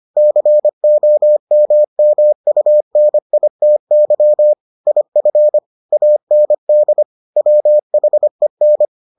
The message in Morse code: COMMUNITY IF AND WHEN

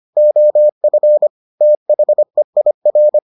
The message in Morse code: OF THEIR

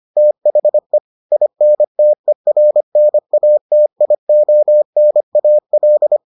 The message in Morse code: THE INTERNATIONAL

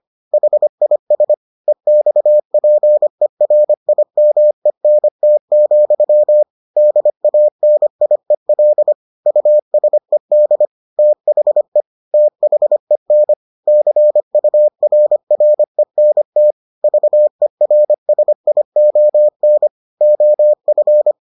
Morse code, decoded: HIS EXPERIMENT, DANIEL USED THE THEN CURRENT VERSION OF

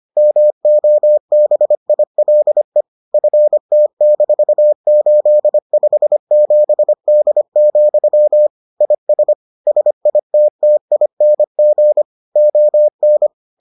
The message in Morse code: MOBILE FT-857D, IS SITTING ON

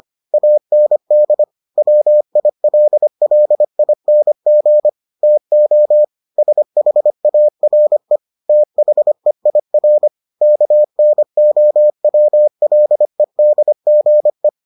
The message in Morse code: AND WILLING TO SHARE THEIR KNOWLEDGE

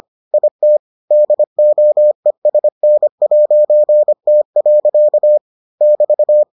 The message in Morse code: IT DOESN'T. <BT>

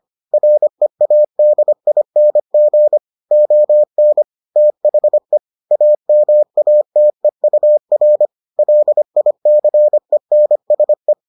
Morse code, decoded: READING ON THE AMATEUR LICENSE

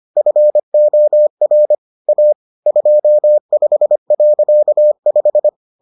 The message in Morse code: FOR A 25.5

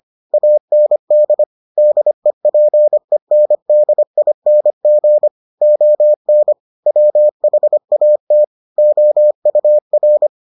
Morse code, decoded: AND DEPENDING ON WHAT OUR

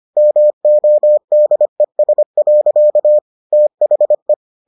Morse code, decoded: MODES. THE

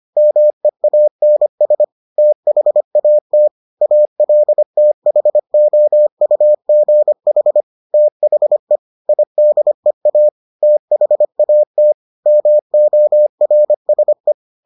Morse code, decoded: MEANS THAT ALTHOUGH THE IDEA THAT MORSE

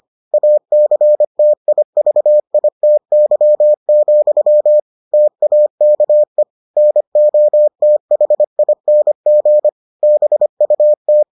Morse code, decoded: ACTIVITY, TAKE NOTHING BUT

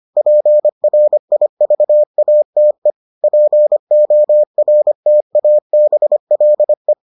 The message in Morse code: PRIVATE PORTABLE

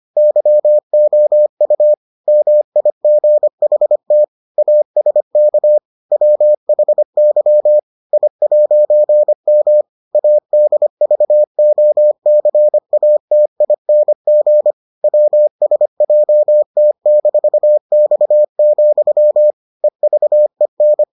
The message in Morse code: YOU MIGHT ASK WHY I'M ADVOCATING WSJT-X, EVEN

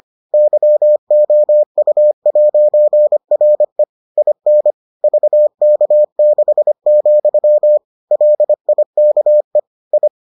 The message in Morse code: YOU'RE IN VK6, LIKE I

